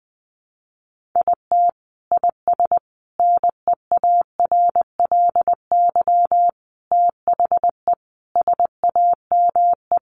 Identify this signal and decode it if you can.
Morse code, heard as a keyed tone, about 20 wpm, 700 Hz. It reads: IT IS NEARLY THE SAME
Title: Sentences from Top 1300 Words 20wpm